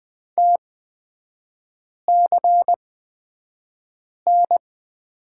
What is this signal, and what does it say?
Morse code, heard as a keyed tone, about 20 wpm, spaced as for 5 wpm, 690 Hz. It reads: TCN
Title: 229 Lesson 37e - Letter x - Three Characters 20wpm